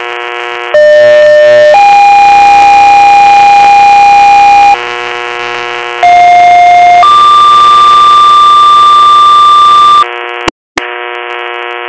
Two-tone page, heard as a one-second tone, then a three-second tone, about 600 then 810 Hz, about 710 then 1180 Hz, with noise. It is heard from 0.7 to 10.0 s.